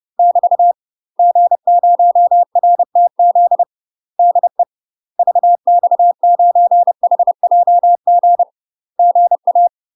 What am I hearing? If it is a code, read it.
Morse code, heard as a keyed tone, about 30 wpm, 700 Hz. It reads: <BT> G0RTZ DE VX9HJG GA
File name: UK MASTER QSOs TRAINING / 30WPM / UK MASTER QSOs 0004